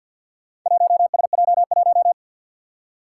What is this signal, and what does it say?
Morse code, heard as a keyed tone, about 51 wpm, 700 Hz. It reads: 1SJ1
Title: Four Challenging Letter-Number Combinations 50wpm